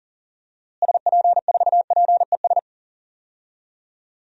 Morse code, heard as a keyed tone, about 40 wpm, 700 Hz. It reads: SP4PES
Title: DXCC-100 Call Signs - 6 Characters - Repeated 2x - 40wpm